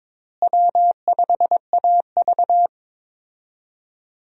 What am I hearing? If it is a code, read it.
Morse code, heard as a keyed tone, about 22 wpm, 700 Hz. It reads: W5AV
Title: USA Call Signs - 4 Characters - Repeated 3x - 22wpm